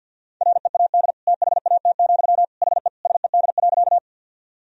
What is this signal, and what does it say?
Morse code, heard as a keyed tone, about 51 wpm, 700 Hz. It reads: READ THAT, HE SED.
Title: Sentences from Top 200 Words and Abbreviations 50wpm